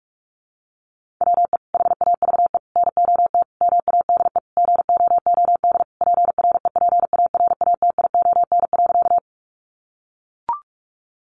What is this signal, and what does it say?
Morse code, heard as a keyed tone, about 44 wpm, 700 Hz. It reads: WE HAVE NOT MADE GOOD PREPARATION.
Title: Sentences from Top 1400 Words 45wpm